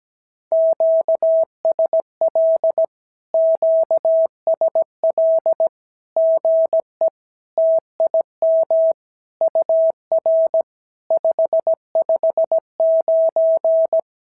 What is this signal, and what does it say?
Morse code, heard as a keyed tone, about 17 wpm, 650 Hz. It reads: QSL QSL GE TIM UR 559
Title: SOTA QSO Practice 17wpm